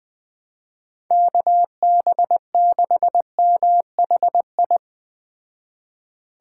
Morse code, heard as a keyed tone, about 20 wpm, 700 Hz. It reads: KB6MHI
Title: Speed Racing - USA 2x3 Call Signs 17wpm